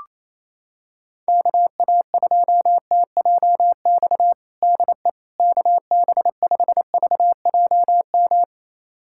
Morse code, heard as a keyed tone, about 28 wpm, 700 Hz. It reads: KA2TJX DE KB5VJM
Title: USA Callsign DE Callsign 28wpm